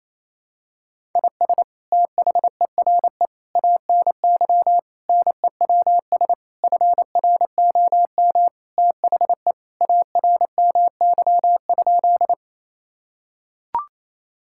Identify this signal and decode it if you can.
Morse code, heard as a keyed tone, about 28 wpm, 700 Hz. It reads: IS THERE ANY NEWS FROM THE ARMY?
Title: Sentences from Top 1000 Words - Review Part 5 - 28wpm